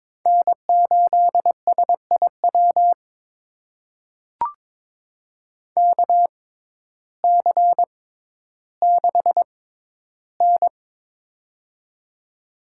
Morse code, heard as a keyed tone, about 22 wpm, spaced as for 12 wpm, 700 Hz. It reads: N8SIW KC6N
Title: USA Call Signs - 6x Character Spacing - 22wpm